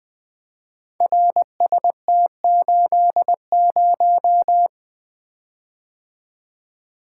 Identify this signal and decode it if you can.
Morse code, heard as a keyed tone, about 20 wpm, 700 Hz. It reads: RST80
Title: Code Groups - 5 Letter:Numbers - 20wpm